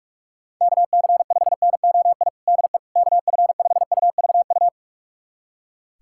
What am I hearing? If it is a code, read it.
Morse code, heard as a keyed tone, about 46 wpm, 700 Hz. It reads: KC5NOI DE KF5UVU
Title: USA Callsign DE Callsign - 6 Characters - 45wpm